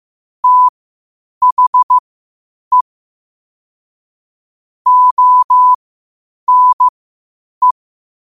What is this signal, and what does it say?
Morse code, heard as a keyed tone, about 15 wpm, spaced as for 8 wpm, 1000 Hz. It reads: THE ONE
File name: qotdfiles / 8wpm